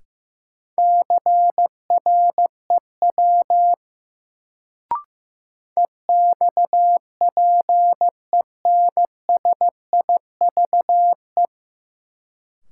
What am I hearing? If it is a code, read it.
Morse code, heard as a keyed tone, about 16 wpm, 700 Hz. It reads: CREW EXPENSIVE
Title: Sentences from Top 1500 Words - New Word Drill Alt - 15wpm